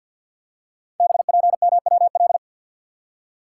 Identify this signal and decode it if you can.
Morse code, heard as a keyed tone, about 53 wpm, 700 Hz. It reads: BPMWL
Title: Code Groups - 5 Letters - 50wpm